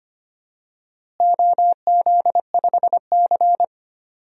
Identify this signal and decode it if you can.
Morse code, heard as a keyed tone, about 25 wpm, 700 Hz. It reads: OZ5C
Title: DXCC-100 Call Signs - 4 Characters - Repeated 2x - 25wpm